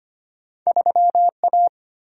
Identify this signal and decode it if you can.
Morse code, heard as a keyed tone, about 25 wpm, 690 Hz. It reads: 3A
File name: Two Letter-Number Combinations - 25wpm